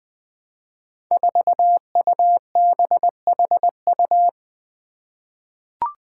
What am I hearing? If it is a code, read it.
Morse code, heard as a keyed tone, about 20 wpm, 700 Hz. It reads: 4UBHU
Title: Five Challenging Letter-Number Combinations 20wpm